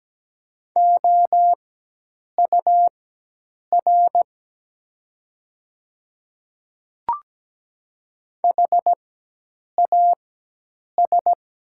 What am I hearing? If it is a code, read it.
Morse code, heard as a keyed tone, about 17 wpm, spaced as for 8 wpm, 700 Hz. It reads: OUR HAS
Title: 4x Character Spacing - Top 300 Words - 17wpm